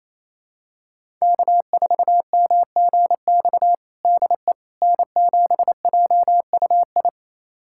Morse code, heard as a keyed tone, about 28 wpm, 700 Hz. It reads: K4MGX DE N7JUI
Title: USA Callsign DE Callsign - 5 Characters - 28wpm